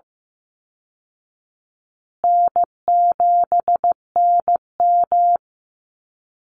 Morse code, heard as a keyed tone, about 15 wpm, 700 Hz. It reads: N7NM
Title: Easy Call Signs Repeated 3x - 15wpm